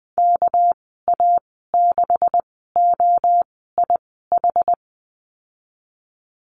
Morse code, heard as a keyed tone, about 20 wpm, spaced as for 15 wpm, 700 Hz. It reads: KA6OIH
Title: 213 Lesson 34g - Number 8 - Call Signs 20wpm